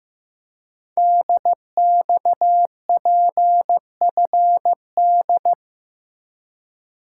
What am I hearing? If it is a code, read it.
Morse code, heard as a keyed tone, about 15 wpm, 700 Hz. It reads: DXPFD
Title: Code Groups - 5 Letters - 15wpm